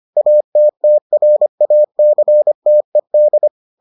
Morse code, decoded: ATTRACTED

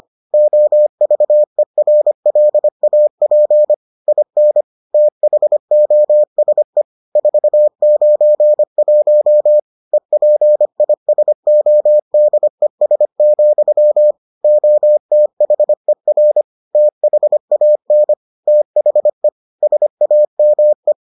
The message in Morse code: OVERLAP IN THOSE 491 EPISODES, OTHER THAN THE SAME

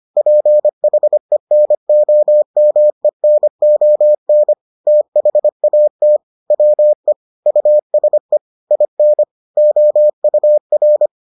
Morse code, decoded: PHENOMENON THAT WE USE IN OUR

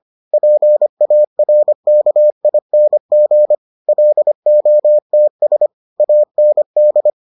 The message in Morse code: PARKING LOTS AND

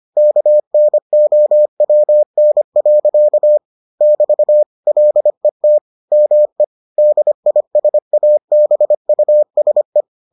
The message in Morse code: KNOWN. <BT> LET ME DISABUSE

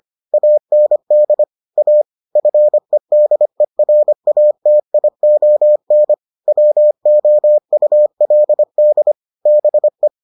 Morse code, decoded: AND A FEDERATION WOULD BE

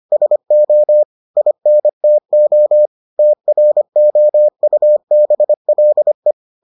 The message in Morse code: SO INTO TROUBLE

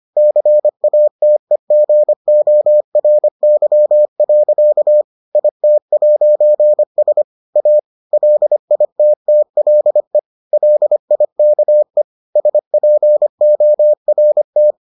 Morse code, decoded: CATEGORY. IT'S A LITTLE LIKE SPORT

